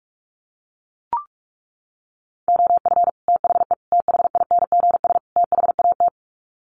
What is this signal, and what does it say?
Morse code, heard as a keyed tone, about 44 wpm, 700 Hz. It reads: OF THE THINGS THAT
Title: Sets of 4 Words 45wpm